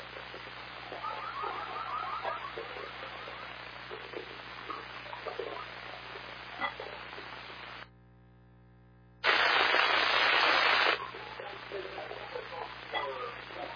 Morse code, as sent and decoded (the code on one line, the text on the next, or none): none